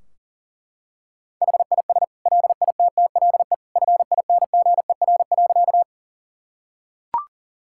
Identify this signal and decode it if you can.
Morse code, heard as a keyed tone, about 40 wpm, 700 Hz. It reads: HIS LITTLE FINGER.
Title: Sentences from Top 1600 Words 40wpm